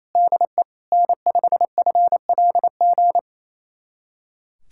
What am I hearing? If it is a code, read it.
Morse code, heard as a keyed tone, about 28 wpm, 700 Hz. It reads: DE N5FLG
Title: USA Callsign DE Callsign - 5 Characters - 28wpm